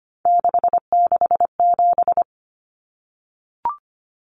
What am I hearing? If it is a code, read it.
Morse code, heard as a keyed tone, about 25 wpm, 700 Hz. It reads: 667
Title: Common Cut Numbers - 3-Digit 25wpm